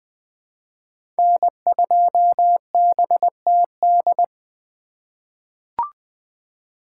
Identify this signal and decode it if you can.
Morse code, heard as a keyed tone, about 20 wpm, 700 Hz. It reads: N2BTD
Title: 2023 POTA Activator Call Signs - 20wpm